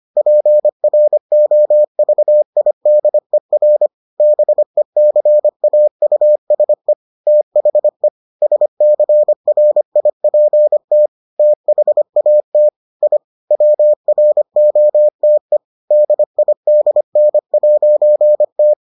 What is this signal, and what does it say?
Morse code, heard as a keyed tone, about 25 wpm, 600 Hz. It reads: PROVIDER BECAUSE THE SCRIPT THAT I WROTE DIDN'T